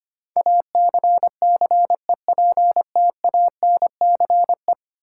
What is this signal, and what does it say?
Morse code, heard as a keyed tone, about 25 wpm, 700 Hz. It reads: ACCEPTANCE
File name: Ten Letter Words - 25wpm